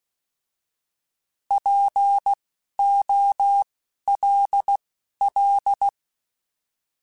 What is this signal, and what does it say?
Morse code, heard as a keyed tone, about 16 wpm, spaced as for 11 wpm, 800 Hz. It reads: POLL
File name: Alphabetical / Alphabetical mp3s / Ps @10